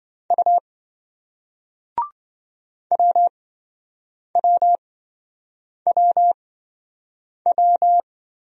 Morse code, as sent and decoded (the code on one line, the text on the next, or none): ..- / .-- .-- .-- .--
U WWWW